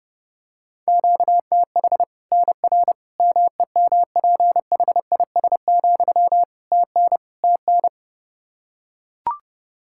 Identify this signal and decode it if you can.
Morse code, heard as a keyed tone, about 30 wpm, 700 Hz. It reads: QTH NR MEMPHIS, TN TN
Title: Practice QSO - Exchange 1 - Beginning - 30wpm